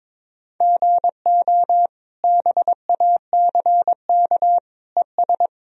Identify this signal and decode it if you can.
Morse code, heard as a keyed tone, about 22 wpm, 700 Hz. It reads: GO BACK ES
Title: Sentences from Top 300 Words and Abbreviations 22wpm